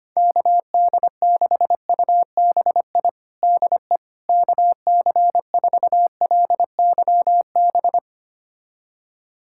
Morse code, heard as a keyed tone, about 25 wpm, 700 Hz. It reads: KD6UBI DE KC4LYB
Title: USA Callsign DE Callsign - 6 Characters - 25wpm